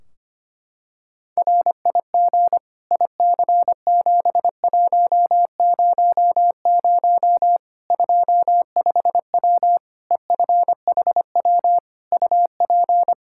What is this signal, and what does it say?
Morse code, heard as a keyed tone, about 25 wpm, 700 Hz. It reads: RIG IC7100 25W EFHW UP
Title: Practice QSO - Exchange 2 - Info - 25wpm